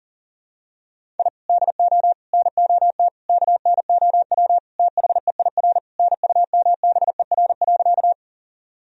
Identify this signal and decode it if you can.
Morse code, heard as a keyed tone, about 40 wpm, 700 Hz. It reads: I DO NOT KNOW THEIR NUMBER.
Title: Sentences from Top 1000 Words - Review Part 1 - 40wpm